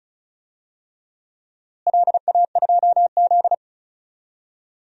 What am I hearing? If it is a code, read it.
Morse code, heard as a keyed tone, about 36 wpm, 700 Hz. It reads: LA2Z
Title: DXCC-100 Call Signs - 4 Characters - Repeated 3x - 35wpm